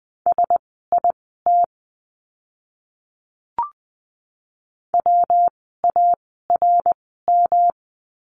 Morse code, heard as a keyed tone, about 20 wpm, spaced as for 14 wpm, 700 Hz. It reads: SIT WARM
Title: 115 Lesson 19f - Letter w - Words 20wpm